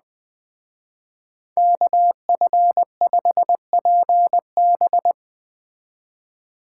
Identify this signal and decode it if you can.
Morse code, heard as a keyed tone, about 19 wpm, 700 Hz. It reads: KF5PB
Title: Speed Racing - USA 2x2 Call Signs 17wpm